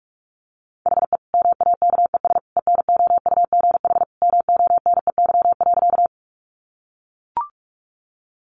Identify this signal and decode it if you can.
Morse code, heard as a keyed tone, about 44 wpm, 700 Hz. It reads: HE MAKES ENOUGH MONEY.
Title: Additional Easy Phrases 45wpm